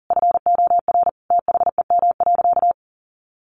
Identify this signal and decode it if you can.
Morse code, heard as a keyed tone, about 40 wpm, 700 Hz. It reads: FOR THEM.
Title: Sentences from Top 1200 Words 40wpm